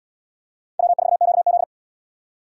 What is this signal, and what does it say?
Morse code, heard as a keyed tone, about 70 wpm, 700 Hz. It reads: W4QZ